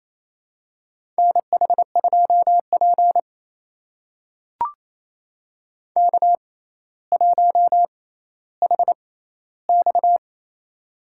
Morse code, heard as a keyed tone, about 28 wpm, spaced as for 14 wpm, 700 Hz. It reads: NH2P K1HX
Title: USA Call Signs - 4 Characters - 6x Character Spacing - 28wpm